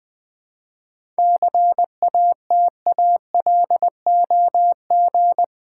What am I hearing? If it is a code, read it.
Morse code, heard as a keyed tone, about 20 wpm, 700 Hz. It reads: CATALOG
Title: 7 Letter Words - First 3 Letters Given - 20wpm